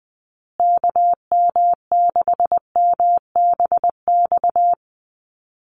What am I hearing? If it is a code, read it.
Morse code, heard as a keyed tone, about 20 wpm, 700 Hz. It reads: KM6MBX
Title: USA Call Signs Repeated 3x 20wpm